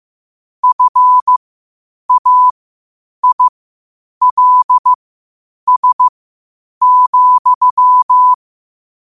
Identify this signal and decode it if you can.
Morse code, heard as a keyed tone, about 15 wpm, spaced as for 8 wpm, 1000 Hz. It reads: FAILS,
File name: qotdfiles / 8wpm